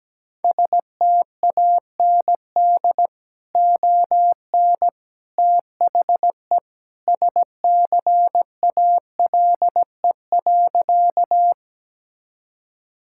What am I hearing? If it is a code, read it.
Morse code, heard as a keyed tone, about 17 wpm, 700 Hz. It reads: STAND ON THE SCALE.
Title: Sentences from Top 1000 Words - Review Part 5 - 17wpm